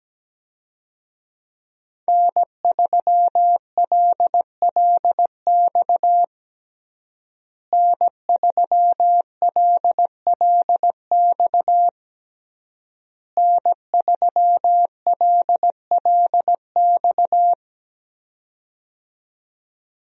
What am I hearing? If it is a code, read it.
Morse code, heard as a keyed tone, about 17 wpm, 700 Hz. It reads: N3LLX N3LLX N3LLX
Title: USA Call Signs - 5 Characters - Repeated 3x - 17wpm